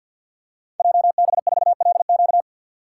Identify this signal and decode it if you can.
Morse code, heard as a keyed tone, about 49 wpm, 700 Hz. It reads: JB4LQ
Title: Code Groups - 5 Letter:Numbers - 50wpm